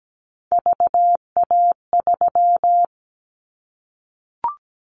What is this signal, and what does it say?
Morse code, heard as a keyed tone, about 17 wpm, 700 Hz. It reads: VA3
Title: Callsign Prefixes for Top 100 DXCC - 17wpm